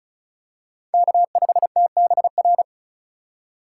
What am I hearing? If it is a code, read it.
Morse code, heard as a keyed tone, about 34 wpm, 700 Hz. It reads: K5TBR
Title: USA Call Signs - 6x Character Spacing - 35wpm